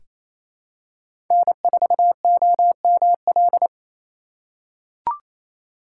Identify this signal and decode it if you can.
Morse code, heard as a keyed tone, about 28 wpm, 700 Hz. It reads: N4OML
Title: USA Call Signs - 3x Character Spacing - 28wpm